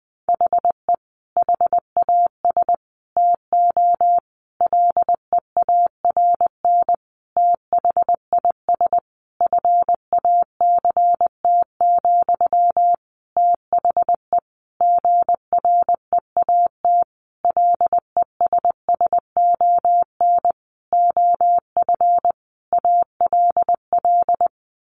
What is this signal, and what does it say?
Morse code, heard as a keyed tone, about 20 wpm, 700 Hz. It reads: HE HAS TO LEARN THIS FACT, THE GREAT LESSON OF ALL
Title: Sentences from Top 1200 Words 20wpm